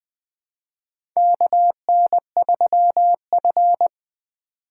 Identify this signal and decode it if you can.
Morse code, heard as a keyed tone, about 20 wpm, 700 Hz. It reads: KN3F
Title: USA Call Signs - 2x Character Spacing - 20wpm